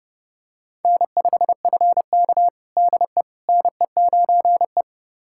Morse code, heard as a keyed tone, about 30 wpm, 700 Hz. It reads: N5FK DE NE9E
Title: USA Callsign DE Callsign - 4 Characters - 30wpm